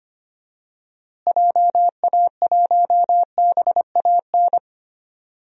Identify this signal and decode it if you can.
Morse code, heard as a keyed tone, about 25 wpm, 700 Hz. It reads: JA1BAN
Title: DXCC-100 Call Signs - 6 Characters - Repeated 2x - 25wpm